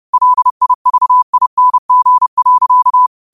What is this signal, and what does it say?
Morse code, heard as a keyed tone, about 30 wpm, 1000 Hz. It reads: LIVING.